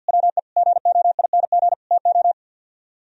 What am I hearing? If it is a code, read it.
Morse code, heard as a keyed tone, about 64 wpm, 700 Hz. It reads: WE GOING TO